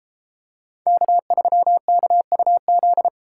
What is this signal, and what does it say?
Morse code, heard as a keyed tone, about 33 wpm, 700 Hz. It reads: K3KUZ